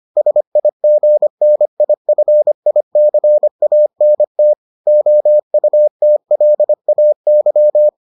Morse code, heard as a keyed tone, about 25 wpm, 600 Hz. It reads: SIGNIFICANT OUTLAY